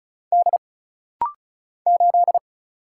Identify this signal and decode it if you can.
Morse code, heard as a keyed tone, about 33 wpm, 700 Hz. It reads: D 8
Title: Single Letter-Number Alt - Rapid-Fire 35wpm